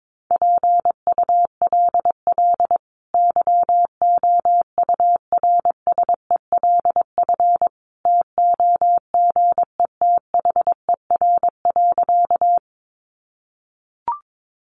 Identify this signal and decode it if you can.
Morse code, heard as a keyed tone, about 22 wpm, 700 Hz. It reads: PULL YOURSELF TOGETHER.
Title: Sentences from Top 600 Words 22wpm